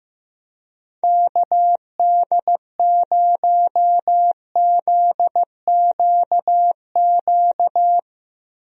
Morse code, heard as a keyed tone, about 15 wpm, 700 Hz. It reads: KD0ZQQ
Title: USA Call Signs - 6 Characters - Repeated 2x - 15wpm